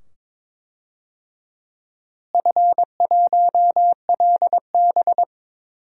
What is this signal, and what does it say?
Morse code, heard as a keyed tone, about 22 wpm, 700 Hz. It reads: F1LB
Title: DXCC-100 Call Signs - 4 Characters - Repeated 2x - 22wpm